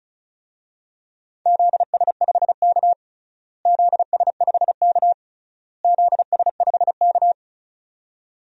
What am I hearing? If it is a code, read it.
Morse code, heard as a keyed tone, about 36 wpm, 700 Hz. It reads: ZS5K ZS5K ZS5K
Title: DXCC-100 Call Signs - 4 Characters - Repeated 3x - 35wpm